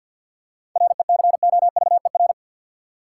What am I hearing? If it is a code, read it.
Morse code, heard as a keyed tone, about 50 wpm, 700 Hz. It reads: RECOVER
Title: Seven Letter Words - 50wpm